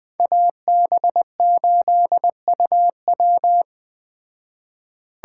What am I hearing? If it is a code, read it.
Morse code, heard as a keyed tone, about 20 wpm, 700 Hz. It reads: AB8UW
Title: Speed Racing - USA 2x2 Call Signs 17wpm